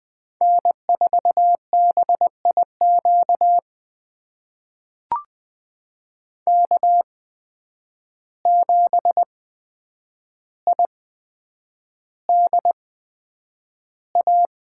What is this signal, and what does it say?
Morse code, heard as a keyed tone, about 20 wpm, spaced as for 9 wpm, 700 Hz. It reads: N4BIQ K7IDA
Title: USA Call Signs - 5 Characters - 8x Character Spacing - 20wpm